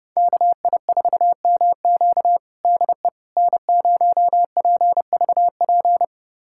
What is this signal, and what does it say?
Morse code, heard as a keyed tone, about 30 wpm, 700 Hz. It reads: KI4MQ DE N0PVP
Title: USA Callsign DE Callsign - 5 Characters - 30wpm